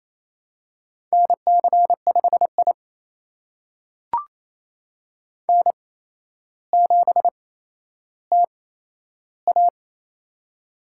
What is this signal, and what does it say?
Morse code, heard as a keyed tone, about 27 wpm, spaced as for 12 wpm, 700 Hz. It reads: NC5I N7TA
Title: USA Call Signs - 4 Characters - 8x Character Spacing - 28wpm